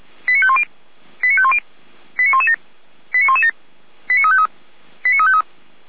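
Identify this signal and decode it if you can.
Five-tone selective call (ZVEI), heard as quick runs of five tones, tones of about 70 ms. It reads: 89510 89510 89117 89117 89353 89353
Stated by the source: Five-tone ZVEI selective call 89167, received by SDR by fire brigade Orlamünde (feuerwehr-orlamuende).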